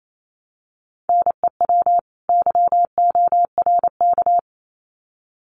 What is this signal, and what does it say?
Morse code, heard as a keyed tone, about 28 wpm, 700 Hz. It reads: NEW YORK
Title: Sets of 2 Words 28wpm